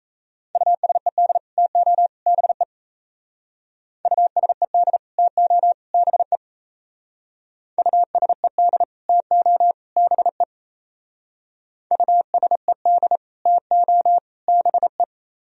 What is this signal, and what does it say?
Morse code, heard as a keyed tone, about 34 wpm, 700 Hz. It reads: USED TO BE USED TO BE USED TO BE USED TO BE
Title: Speed-Racing - Sets of 3 Words 28wpm